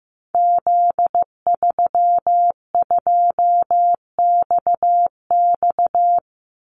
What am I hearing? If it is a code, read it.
Morse code, heard as a keyed tone, about 15 wpm, 700 Hz. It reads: Z32XX
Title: Call Signs Repeated 2x - 15wpm